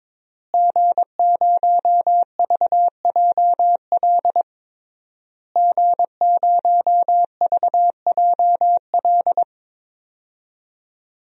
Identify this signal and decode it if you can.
Morse code, heard as a keyed tone, about 21 wpm, 700 Hz. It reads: G0VJL G0VJL
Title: DXCC-100 Call Signs - 5 Characters - Repeated 3x - 22wpm